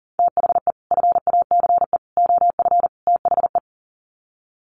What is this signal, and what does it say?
Morse code, heard as a keyed tone, about 40 wpm, 700 Hz. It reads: THE FACE OF THE
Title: Sets of 4 Words 40wpm